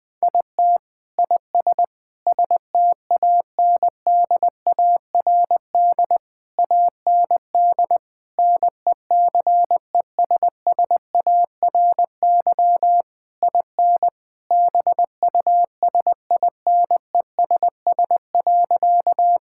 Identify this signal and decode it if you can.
Morse code, heard as a keyed tone, about 20 wpm, 700 Hz. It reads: IT IS STANDARD AND NECESSARY IN BUSINESS.
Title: Sentences from Top 1000 Words - Review Part 6 - 20wpm